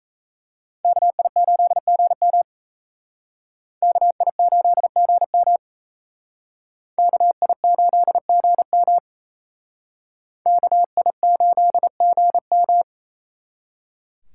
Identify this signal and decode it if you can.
Morse code, heard as a keyed tone, about 36 wpm, 700 Hz. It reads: KI8GM KI8GM KI8GM KI8GM